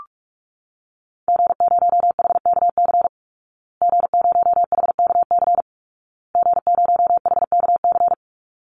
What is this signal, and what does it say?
Morse code, heard as a keyed tone, about 44 wpm, 700 Hz. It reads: G0HKC G0HKC G0HKC
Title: Call Signs Repeated 2x - 45wpm